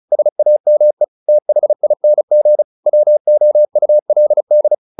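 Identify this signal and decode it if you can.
Morse code, heard as a keyed tone, about 34 wpm, 600 Hz. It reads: SAME THING WOULD
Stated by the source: Morse practice at 35 wpm from CW med Gnister, sending Ken Follett's novel Never in English.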